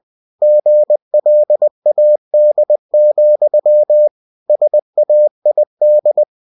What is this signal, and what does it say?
Morse code, heard as a keyed tone, about 20 wpm, 600 Hz. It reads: GLAD, SAID
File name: cw-books / winnie-the-pooh / winnie-20wpm-page-0058